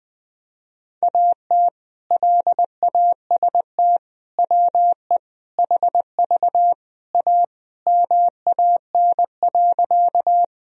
Morse code, heard as a keyed tone, about 20 wpm, 700 Hz. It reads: AT LAST WE HV A MAN.
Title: Sentences from Top 300 Words and Abbreviations 20wpm